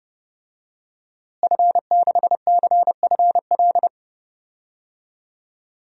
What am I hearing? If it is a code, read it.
Morse code, heard as a keyed tone, about 32 wpm, 700 Hz. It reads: F6CFL
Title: DXCC-100 Call Signs - 5 Characters - Repeated 2x - 30wpm